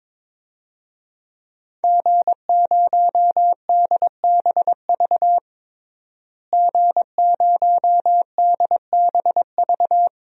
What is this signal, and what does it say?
Morse code, heard as a keyed tone, about 22 wpm, 700 Hz. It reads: G0DBV G0DBV
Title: DXCC-100 Call Signs - 5 Characters - Repeated 2x - 22wpm